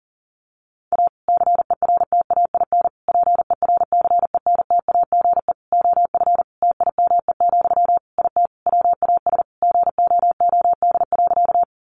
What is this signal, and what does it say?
Morse code, heard as a keyed tone, about 40 wpm, 700 Hz. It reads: A CERTAIN PERCENTAGE OF TIME, IT WAS GOOD.
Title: Sentences from Top 1300 Words 40wpm